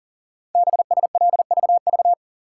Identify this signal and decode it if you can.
Morse code, heard as a keyed tone, about 40 wpm, 700 Hz. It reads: BSLVV
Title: Code Groups - 5 Letters - 40wpm